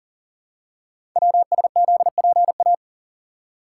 Morse code, heard as a keyed tone, about 38 wpm, 700 Hz. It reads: WSZPA
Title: Code Groups - 5 Letters - 40wpm